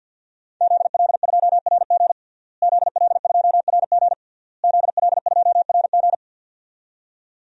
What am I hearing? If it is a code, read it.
Morse code, heard as a keyed tone, about 50 wpm, 700 Hz. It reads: ZL2RG ZL2RG ZL2RG
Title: DXCC-100 Call Signs - 5 Characters - Repeated 3x - 50wpm